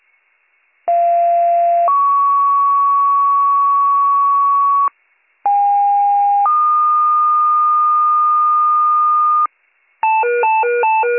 Two-tone page, heard as a one-second tone, then a three-second tone, about 690 then 1120 Hz, about 810 then 1220 Hz. It is heard from 0.9 to 9.5 s.